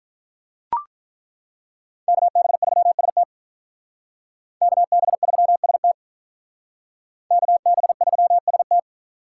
Morse code, heard as a keyed tone, about 42 wpm, 700 Hz. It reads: KB3ST KB3ST KB3ST
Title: Speed Racing - USA 2x2 Call Signs 35wpm